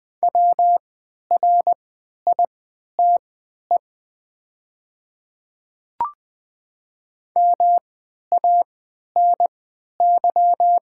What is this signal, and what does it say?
Morse code, heard as a keyed tone, about 20 wpm, spaced as for 11 wpm, 700 Hz. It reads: WRITE MANY
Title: 3x Character Spacing - Top 100 Words - 20wpm